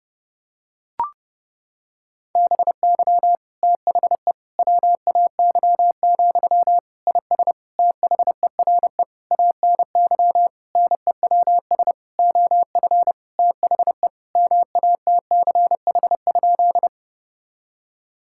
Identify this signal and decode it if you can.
Morse code, heard as a keyed tone, about 30 wpm, 700 Hz. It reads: BY THE WAY, IS THERE ANY NEWS OF THE MATCH?
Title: Sentences from Top 1600 Words 30wpm